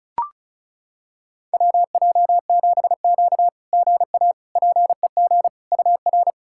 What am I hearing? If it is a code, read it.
Morse code, heard as a keyed tone, about 36 wpm, 700 Hz. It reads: WJ7Q GA PEG UR
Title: SOTA QSO Practice 35wpm